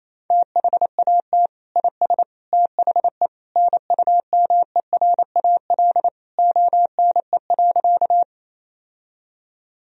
Morse code, heard as a keyed tone, about 28 wpm, 700 Hz. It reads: THAT IS THE NUMERAL ONE.
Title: Sentences from Top 1000 Words - Review Part 2 - 28wpm